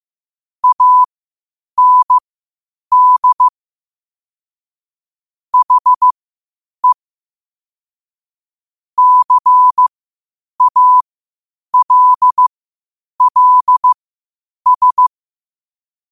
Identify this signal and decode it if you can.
Morse code, heard as a keyed tone, about 15 wpm, spaced as for 8 wpm, 1000 Hz. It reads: AND HE CALLS
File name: qotdfiles / 8wpm